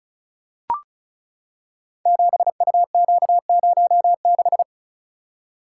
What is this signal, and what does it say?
Morse code, heard as a keyed tone, about 36 wpm, 700 Hz. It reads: 7UQ06
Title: Code Groups - 5 Letter:Numbers - 35wpm